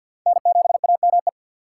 Morse code, heard as a keyed tone, about 46 wpm, 700 Hz. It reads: N7AME
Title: K1USN SST QSO Practice - 50wpm